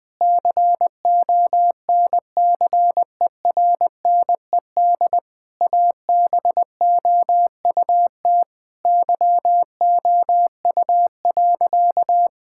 Morse code, heard as a keyed tone, about 20 wpm, 700 Hz. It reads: CONCERNED ABOUT YOU.
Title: Sentences from Top 1000 Words - Review Part 6 - 20wpm